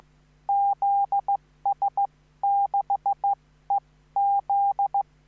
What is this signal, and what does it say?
Morse code, heard as a keyed tone, about 15 wpm, spaced as for 13 wpm, 800 Hz. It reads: ZS6EZ
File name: QSO03